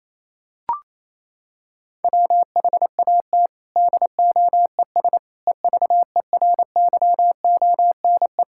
Morse code, heard as a keyed tone, about 28 wpm, 700 Hz. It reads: WHAT DOES EVERYONE
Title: Sentences from Top 1000 Words - Review Part 5 - 28wpm